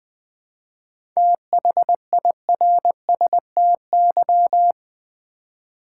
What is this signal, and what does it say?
Morse code, heard as a keyed tone, about 20 wpm, 700 Hz. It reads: THIRSTY
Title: 7 Letter Words - First 3 Letters Given - 20wpm